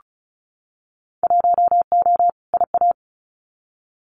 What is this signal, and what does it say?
Morse code, heard as a keyed tone, about 34 wpm, 700 Hz. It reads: 1O IA
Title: Winter Field Day Exchanges 35wpm